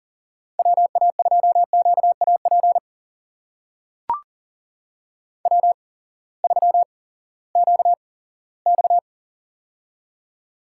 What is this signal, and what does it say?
Morse code, heard as a keyed tone, about 40 wpm, spaced as for 20 wpm, 700 Hz. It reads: WA2QAP W3QX
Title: USA Call Signs - 8x Character Spacing - 40wpm